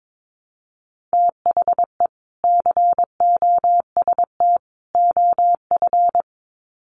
Morse code, heard as a keyed tone, about 22 wpm, 700 Hz. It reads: THE COST OF
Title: Sets of 3 Words 22wpm